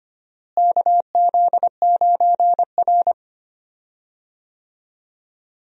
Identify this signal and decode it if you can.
Morse code, heard as a keyed tone, about 25 wpm, 700 Hz. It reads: KZ9R
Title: USA Call Signs - 4 Characters - Repeated 3x - 25wpm